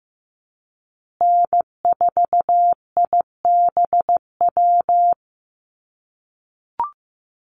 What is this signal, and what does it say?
Morse code, heard as a keyed tone, about 15 wpm, 700 Hz. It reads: N4IBW
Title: USA Call Signs Repeated 3x 15wpm